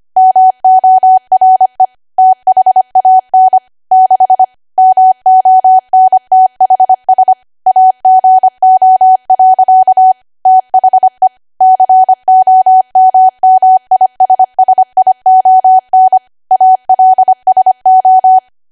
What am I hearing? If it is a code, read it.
Morse code, heard as a keyed tone, about 25 wpm, 750 Hz. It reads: MORE THAN 6 MONTHS AGO. THE COMMISSION ALSO
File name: cwa / W1AW-Level3 / 25WPM